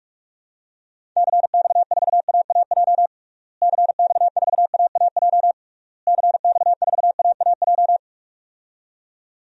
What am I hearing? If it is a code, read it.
Morse code, heard as a keyed tone, about 46 wpm, 700 Hz. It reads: CX4AAJ CX4AAJ CX4AAJ
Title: DXCC-100 Call Signs - 6 Characters - Repeated 3x - 45wpm